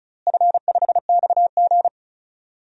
Morse code, heard as a keyed tone, about 36 wpm, 700 Hz. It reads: F5XG
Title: DXCC-100 Call Signs - 4 Characters - Repeated 3x - 35wpm